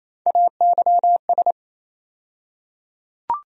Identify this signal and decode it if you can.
Morse code, heard as a keyed tone, about 28 wpm, 700 Hz. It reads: AYS
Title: USA Call Sign Postfixes - 3 Letters 28wpm